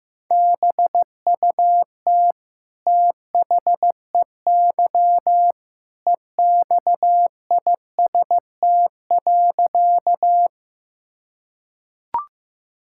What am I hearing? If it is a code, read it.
Morse code, heard as a keyed tone, about 16 wpm, 700 Hz. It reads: BUT THEY EXIST.
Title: Sentences from Top 1000 Words - Review Part 5 - 15wpm